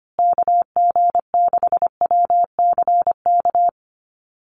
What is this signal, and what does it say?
Morse code, heard as a keyed tone, about 25 wpm, 700 Hz. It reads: KG6WCK
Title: USA Call Signs Repeated 3x 25wpm